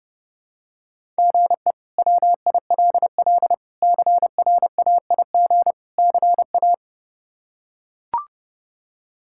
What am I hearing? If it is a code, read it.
Morse code, heard as a keyed tone, about 29 wpm, 700 Hz. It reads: GE WILL CRAIG CA
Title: K1USN SST QSO Practice - 30wpm